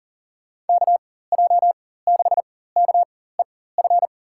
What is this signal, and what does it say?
Morse code, heard as a keyed tone, about 41 wpm, spaced as for 19 wpm, 700 Hz. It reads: KJ6KEF
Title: USA Call Signs - 6 Characters - 4x Character Spacing - 40wpm